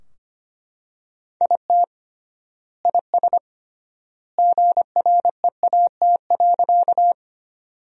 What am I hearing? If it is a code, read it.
Morse code, heard as a keyed tone, about 25 wpm, 700 Hz. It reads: IT IS GREAT.
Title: 3x Word-Spacing - Sentences from Top 200 Words - 25wpm